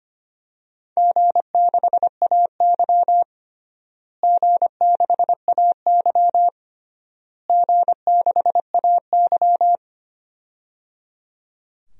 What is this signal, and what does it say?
Morse code, heard as a keyed tone, about 25 wpm, 700 Hz. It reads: G6AY G6AY G6AY
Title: DXCC-100 Call Signs - 4 Characters - Repeated 3x - 25wpm